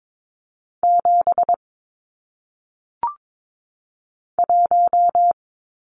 Morse code, heard as a keyed tone, about 22 wpm, 700 Hz. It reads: 7 1
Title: Numbers - 1-Digit 22wpm